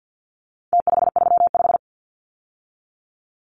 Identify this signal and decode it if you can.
Morse code, heard as a keyed tone, about 50 wpm, 700 Hz. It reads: T535
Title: Numbers - Contest Style - 4-Digit 50wpm